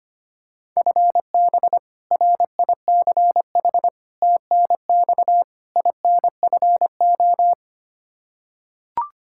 Morse code, heard as a keyed tone, about 25 wpm, 700 Hz. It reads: FB RICH TNX INFO
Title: Practice QSO - Exchange 3 - More Info - 25wpm